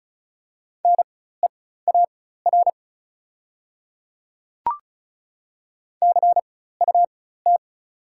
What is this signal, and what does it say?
Morse code, heard as a keyed tone, about 36 wpm, spaced as for 16 wpm, 700 Hz. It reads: NEAR CUT
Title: 4x Character Spacing - Top 300 Words - 35wpm